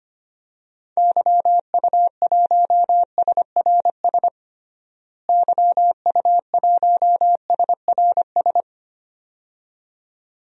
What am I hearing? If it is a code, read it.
Morse code, heard as a keyed tone, about 25 wpm, 700 Hz. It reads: YU1SRS YU1SRS
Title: DXCC-100 Call Signs - 6 Characters - Repeated 2x - 25wpm